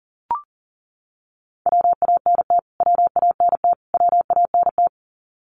Morse code, heard as a keyed tone, about 40 wpm, 700 Hz. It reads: WANT WANT WANT
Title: Top 500 Words Repeated 40wpm